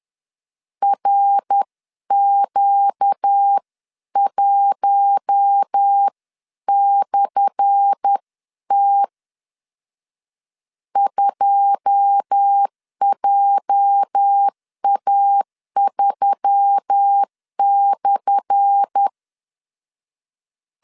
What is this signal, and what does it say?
Morse code, heard as a keyed tone, about 10 wpm, 800 Hz. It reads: RQ1/T 2JA3/
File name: LEZIONI / 10 SIGNS / 10A SLASH 10wpm 2808